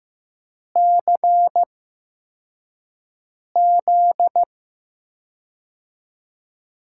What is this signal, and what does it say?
Morse code, heard as a keyed tone, about 15 wpm, spaced as for 4 wpm, 700 Hz. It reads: CZ
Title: USA Call Signs - 4 Characters - 8x Character Spacing - 15wpm